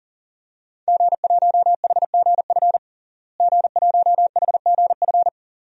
Morse code, heard as a keyed tone, about 40 wpm, 700 Hz. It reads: G1HGF G1HGF
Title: DXCC-100 Call Signs - 5 Characters - Repeated 2x - 40wpm